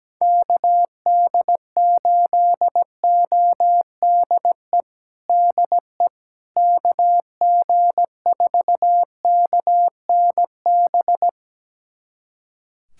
Morse code, heard as a keyed tone, about 17 wpm, 700 Hz. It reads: KD8ODE DE KG4KNB
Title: USA Callsign DE Callsign - 6 Characters - 17wpm